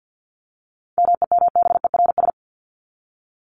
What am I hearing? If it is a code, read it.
Morse code, heard as a keyed tone, about 50 wpm, 700 Hz. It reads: MEMBERS
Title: ES:S Suffix - Top 100 Words with Base Words 50wpm